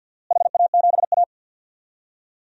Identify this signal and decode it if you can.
Morse code, heard as a keyed tone, about 51 wpm, 700 Hz. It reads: HA7A
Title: DXCC-100 Call Signs - 4 Characters - Repeated 3x - 50wpm